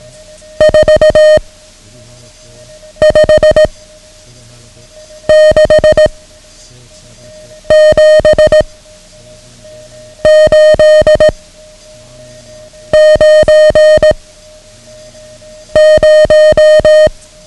Morse code, heard as a keyed tone, about 18 wpm, 620 Hz. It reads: 4 5 6 7 8 9 0
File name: ham radio / CodeQuick / Code Quick 09